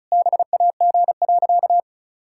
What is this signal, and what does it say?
Morse code, heard as a keyed tone, about 36 wpm, 700 Hz. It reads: BAG.
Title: Pangram Challenge v2 - 35wpm